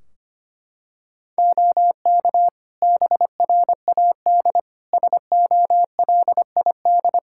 Morse code, heard as a keyed tone, about 25 wpm, 700 Hz. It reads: OK BRAD SOLID
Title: K1USN SST QSO Practice 25wpm